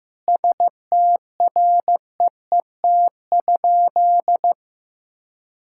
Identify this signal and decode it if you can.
Morse code, heard as a keyed tone, about 15 wpm, 700 Hz. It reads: STREET?
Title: Sentences from Top 1600 Words 15wpm